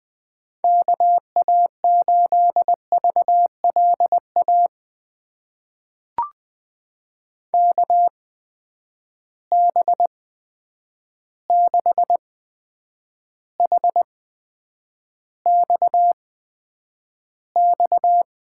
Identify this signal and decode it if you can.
Morse code, heard as a keyed tone, about 20 wpm, spaced as for 9 wpm, 700 Hz. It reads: KA8VLA KB6HXX
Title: USA Call Signs - 6 Characters - 8x Character Spacing - 20wpm